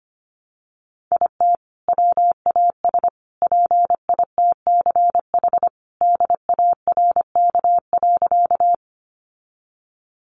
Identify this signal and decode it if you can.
Morse code, heard as a keyed tone, about 25 wpm, 700 Hz. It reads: IT WAS PITCH DARK.
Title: Sentences from Top 1300 Words 25wpm